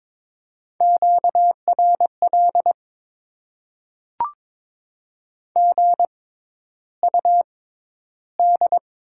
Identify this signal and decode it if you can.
Morse code, heard as a keyed tone, about 22 wpm, spaced as for 11 wpm, 690 Hz. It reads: QRL GUD
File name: Top 20 QSO Elements - 6x Character Spacing - 22wpm